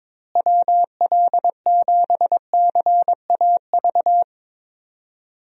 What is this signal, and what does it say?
Morse code, heard as a keyed tone, about 21 wpm, 700 Hz. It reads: WL7CAV
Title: USA Call Signs - 6 Characters - Repeated 3x - 22wpm